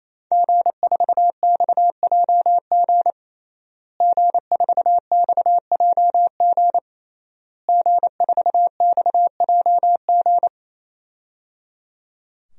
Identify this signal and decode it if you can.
Morse code, heard as a keyed tone, about 28 wpm, 690 Hz. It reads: G4XJG G4XJG G4XJG
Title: DXCC-100 Call Signs - 5 Characters - Repeated 3x - 28wpm